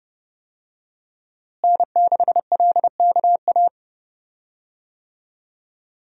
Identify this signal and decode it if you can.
Morse code, heard as a keyed tone, about 31 wpm, 700 Hz. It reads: N6LKA
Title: USA Call Signs - 5 Characters - Repeated 3x - 30wpm